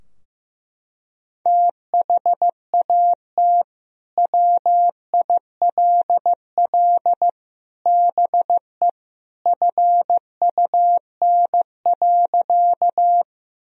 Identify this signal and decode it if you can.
Morse code, heard as a keyed tone, about 15 wpm, 700 Hz. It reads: THAT WILL BE FUN.
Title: Sentences from Top 1000 Words - Review Part 4 - 15wpm